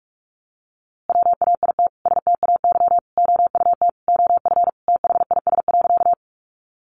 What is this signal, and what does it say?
Morse code, heard as a keyed tone, about 44 wpm, 700 Hz. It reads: WAIT STAY OUT OF THIS.
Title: A Sound Of Thunder 45wpm